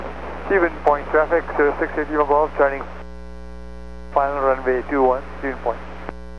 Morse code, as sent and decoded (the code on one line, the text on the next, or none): none